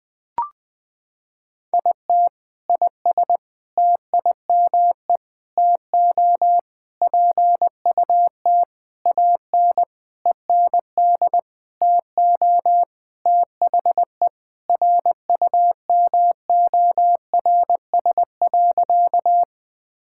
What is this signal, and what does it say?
Morse code, heard as a keyed tone, about 21 wpm, 700 Hz. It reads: IT IS TIME TO PUT AN END TO THE RUMORS.
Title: Sentences with 6-Word Phrases - 20wpm